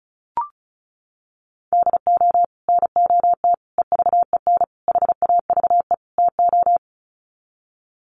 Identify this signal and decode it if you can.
Morse code, heard as a keyed tone, about 34 wpm, 700 Hz. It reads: DO NOT EVEN HAVE TO
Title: Sets of 5 Words Encore2 35wpm